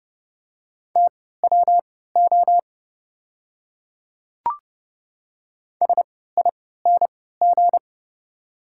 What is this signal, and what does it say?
Morse code, heard as a keyed tone, about 31 wpm, spaced as for 17 wpm, 690 Hz. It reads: TWO SING
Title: 3x Character Spacing - Top 500 Words - 30wpm